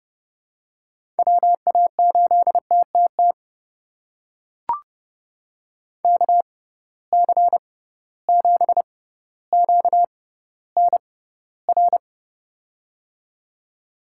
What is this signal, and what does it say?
Morse code, heard as a keyed tone, about 30 wpm, spaced as for 15 wpm, 700 Hz. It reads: WA8TTT KC7QNR
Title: USA Call Signs - 6 Characters - 6x Character Spacing - 30wpm